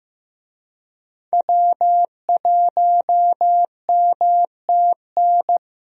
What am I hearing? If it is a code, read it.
Morse code, heard as a keyed tone, about 15 wpm, 700 Hz. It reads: W1MTN
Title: USA Call Signs - 5 Characters - Repeated 3x - 15wpm